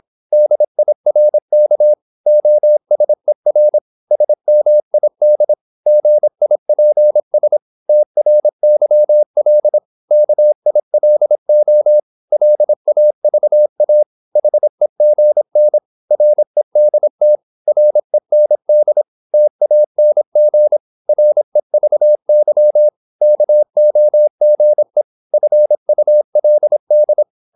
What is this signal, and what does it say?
Morse code, heard as a keyed tone, about 26 wpm, 600 Hz. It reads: DIRK OSER SMID GIPS TRYL KILO LAVA HEGN REDT REND TANG REVY KOGE FULD
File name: Korte ord DK 26wpm